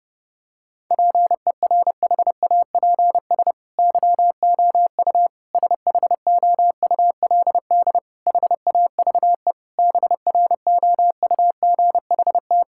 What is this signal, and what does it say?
Morse code, heard as a keyed tone, about 30 wpm, 700 Hz. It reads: PERHAPS YOU SHOULD HAVE BROUGHT
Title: Sentences from Top 1000 Words - Review Part 6 - 30wpm